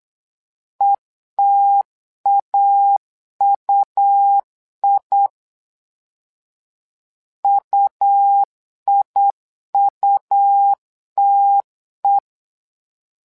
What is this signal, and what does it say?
Morse code, heard as a keyed tone, about 8 wpm, 800 Hz. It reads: ETAUI UIUTE
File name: LEZIONI / 2 UHDN / 2 U 8wpm 2028